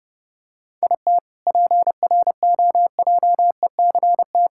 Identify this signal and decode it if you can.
Morse code, heard as a keyed tone, about 30 wpm, 700 Hz. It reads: IT PROJECT
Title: K1USN SST QSO Practice 30wpm